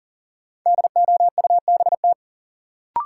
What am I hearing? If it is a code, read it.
Morse code, heard as a keyed tone, about 40 wpm, 700 Hz. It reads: DOUBT
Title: Sentences from Top 1500 Words - New Word Drill Alt - 40wpm